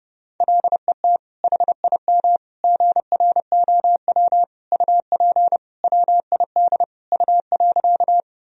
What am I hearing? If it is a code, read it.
Morse code, heard as a keyed tone, about 30 wpm, 700 Hz. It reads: LET HIM GROW UP WID U.
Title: Sentences from Top 300 Words and Abbreviations 30wpm